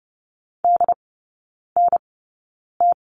Morse code, heard as a keyed tone, about 30 wpm, spaced as for 9 wpm, 700 Hz. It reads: DNT
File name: course / 30:10wpm / 087 Lesson 15e - Number 5 - Three Characters 30wpm